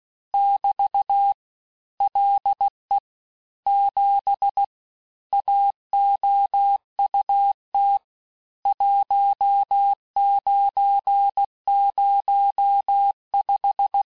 Morse code, heard as a keyed tone, about 16 wpm, 800 Hz. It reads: <BT> LE 7 AOUT 1905
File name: teste-clair-vitesse-15mpm-puis-20mpm